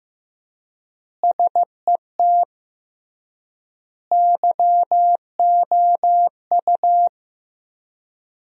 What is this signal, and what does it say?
Morse code, heard as a keyed tone, about 15 wpm, 700 Hz. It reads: SET YOU
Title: 3x Word-Spacing - Sentences from Top 200 Words - 15wpm